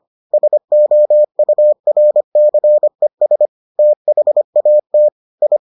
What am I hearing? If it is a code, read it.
Morse code, heard as a keyed tone, about 25 wpm, 600 Hz. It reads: SOURCES THAT I